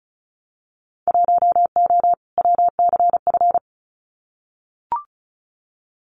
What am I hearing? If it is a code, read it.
Morse code, heard as a keyed tone, about 34 wpm, 690 Hz. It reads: 1O WCF
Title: Winter Field Day Exchanges 35wpm